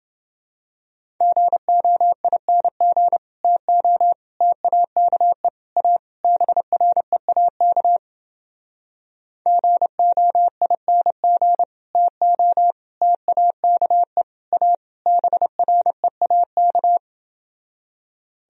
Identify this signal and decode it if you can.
Morse code, heard as a keyed tone, about 29 wpm, 700 Hz. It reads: GOING TO TAKE A BREAK GOING TO TAKE A BREAK